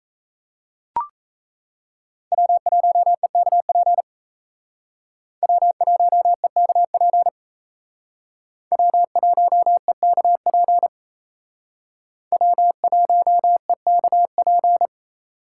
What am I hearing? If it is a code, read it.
Morse code, heard as a keyed tone, about 35 wpm, 700 Hz. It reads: W1EKP W1EKP W1EKP W1EKP